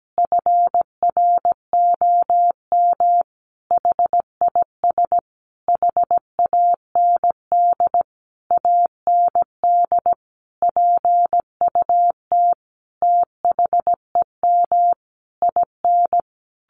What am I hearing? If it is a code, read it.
Morse code, heard as a keyed tone, about 17 wpm, 700 Hz. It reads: FROM HIS HAND AND PUT THEM IN
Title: Sentences from Top 1300 Words 17wpm